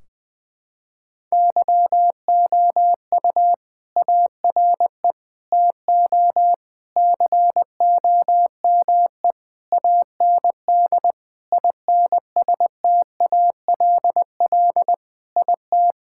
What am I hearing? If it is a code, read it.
Morse code, heard as a keyed tone, about 20 wpm, 700 Hz. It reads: YOU ARE TO COME AND INSTALL IT
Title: Sentences from Top 1600 Words 20wpm